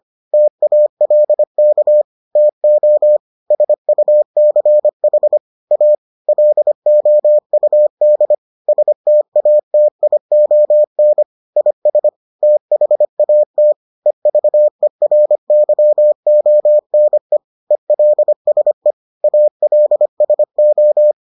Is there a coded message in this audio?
Yes: Morse code from 0.3 to 21.1 s, heard as a keyed tone, about 25 wpm, 600 Hz. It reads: TALK TO SUCH A LOUD STATION IS THAT EVERYONE ELSE ALSO